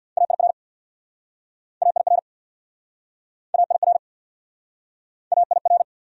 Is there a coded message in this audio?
Yes: Morse code from 0.2 to 5.8 s, heard as a keyed tone, about 67 wpm, 700 Hz. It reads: AIR AIR AIR AIR